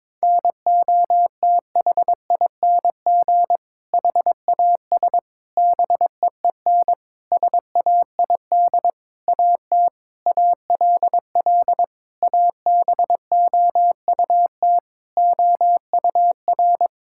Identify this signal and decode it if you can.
Morse code, heard as a keyed tone, about 22 wpm, 700 Hz. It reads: NOTHING HAS BEEN SAID AT ALL ABOUT OUR
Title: Sentences from Top 1000 Words - Review Part 5 - 22wpm